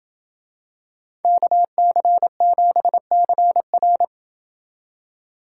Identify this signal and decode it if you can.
Morse code, heard as a keyed tone, about 28 wpm, 700 Hz. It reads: KC7CR